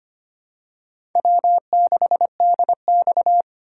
Morse code, heard as a keyed tone, about 25 wpm, 700 Hz. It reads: W6DX
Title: USA Call Signs - 4 Characters - 8x Character Spacing - 25wpm